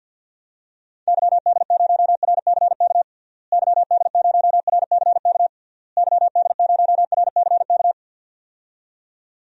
Morse code, heard as a keyed tone, about 47 wpm, 700 Hz. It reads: YD0RCK YD0RCK YD0RCK
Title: DXCC-100 Call Signs - 6 Characters - Repeated 3x - 50wpm